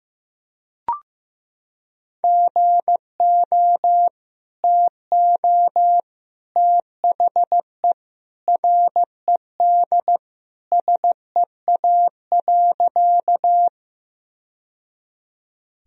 Morse code, heard as a keyed tone, about 16 wpm, 700 Hz. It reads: GO TO THE RED SEA.
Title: Sentences from Top 1000 Words - Review Part 2 - 15wpm